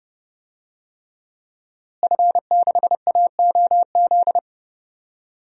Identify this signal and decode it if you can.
Morse code, heard as a keyed tone, about 30 wpm, 700 Hz. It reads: F6AOZ
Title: DXCC-100 Call Signs - 5 Characters - Repeated 3x - 30wpm